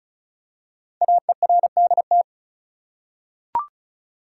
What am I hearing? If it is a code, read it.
Morse code, heard as a keyed tone, about 36 wpm, 700 Hz. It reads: AERDT
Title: Code Groups - 5 Letters - 35wpm